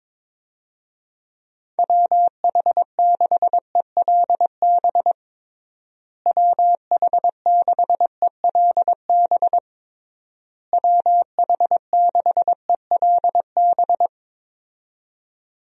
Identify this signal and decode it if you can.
Morse code, heard as a keyed tone, about 22 wpm, 700 Hz. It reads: WH6ELB WH6ELB WH6ELB
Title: USA Call Signs - 6 Characters - Repeated 3x - 22wpm